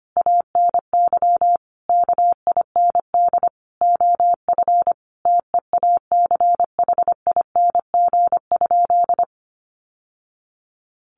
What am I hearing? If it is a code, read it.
Morse code, heard as a keyed tone, about 25 wpm, 700 Hz. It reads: ANY KIND OF TEACHING?
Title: Sentences from Top 1100 Words 25wpm